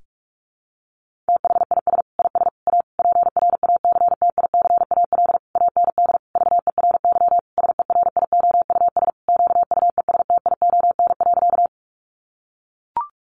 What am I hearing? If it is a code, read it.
Morse code, heard as a keyed tone, about 44 wpm, 700 Hz. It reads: THIS IS A PRACTICAL AND VERY SERIOUS QUESTION.
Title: Sentences from Top 1400 Words 45wpm